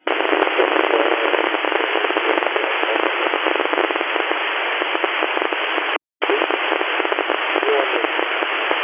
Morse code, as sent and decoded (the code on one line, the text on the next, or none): none